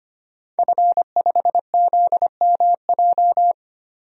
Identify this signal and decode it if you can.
Morse code, heard as a keyed tone, about 25 wpm, 700 Hz. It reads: F5ZMJ
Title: DXCC-100 Call Signs - 5 Characters - 25wpm